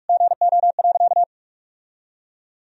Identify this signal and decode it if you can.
Morse code, heard as a keyed tone, about 46 wpm, 700 Hz. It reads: GO.
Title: Sentences from Top 1000 Words - Review Part 2 - 45wpm